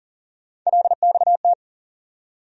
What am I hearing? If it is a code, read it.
Morse code, heard as a keyed tone, about 40 wpm, 700 Hz. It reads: LXT
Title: USA Call Sign Postfixes - 3 Letters 40wpm